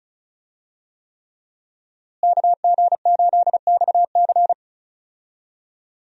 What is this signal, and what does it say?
Morse code, heard as a keyed tone, about 36 wpm, 700 Hz. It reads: KG8XC
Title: USA Call Signs - 5 Characters - Repeated 3x - 35wpm